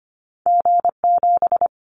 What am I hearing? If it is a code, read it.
Morse code, heard as a keyed tone, about 25 wpm, 700 Hz. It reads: G7
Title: Callsign Prefixes for Top 25 DXCC - 25wpm